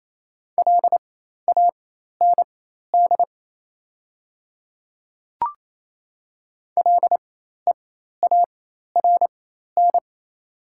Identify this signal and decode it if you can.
Morse code, heard as a keyed tone, about 28 wpm, spaced as for 13 wpm, 700 Hz. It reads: LAND LEARN
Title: 4x Character Spacing - Top 300 Words - 28wpm